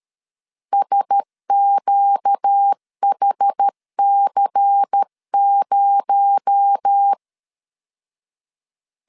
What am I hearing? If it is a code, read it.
Morse code, heard as a keyed tone, about 13 wpm, 800 Hz. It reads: SQHC0
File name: CW / DET / 1 6 1500C S A 12wpm 3440